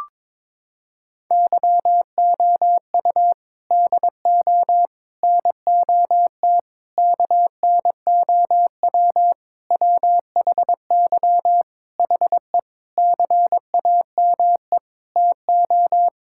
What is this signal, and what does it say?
Morse code, heard as a keyed tone, about 22 wpm, 700 Hz. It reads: YOU DO NOT KNOW WHY HE CAME TO
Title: Sentences from Top 1000 Words - Review Part 6 - 22wpm